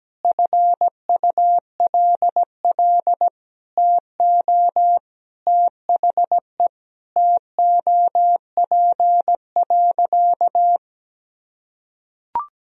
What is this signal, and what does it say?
Morse code, heard as a keyed tone, about 17 wpm, 700 Hz. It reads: FULL TO THE TOP.
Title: Sentences from Top 1000 Words - Review Part 3 - 17wpm